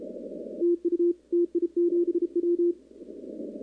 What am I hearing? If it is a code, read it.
Morse code, heard as a keyed tone, about 31 wpm, 340 Hz. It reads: TU TI7W
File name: contests / 2019-cw-wpx-kt5j-k5pi- -2330 0-11